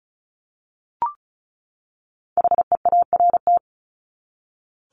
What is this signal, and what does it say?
Morse code, heard as a keyed tone, about 34 wpm, 700 Hz. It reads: HEART
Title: Top 1001-1250 Words 35wpm